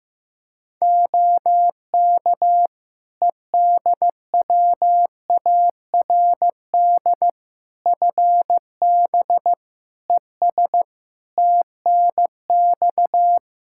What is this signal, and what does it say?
Morse code, heard as a keyed tone, about 15 wpm, 700 Hz. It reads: OK EDWARD FB ES TNX
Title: Practice QSO - Exchange 2 - Info - 15wpm